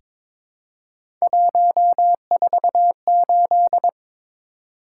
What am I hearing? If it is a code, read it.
Morse code, heard as a keyed tone, about 22 wpm, 700 Hz. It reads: 148
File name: practice / 22wpm / Three Letter-Number Combinations - 22wpm